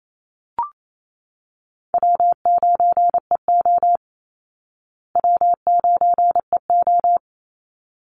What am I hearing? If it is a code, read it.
Morse code, heard as a keyed tone, about 28 wpm, 700 Hz. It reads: W9EO W9EO
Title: Easy Call Signs Repeated 2x - 28wpm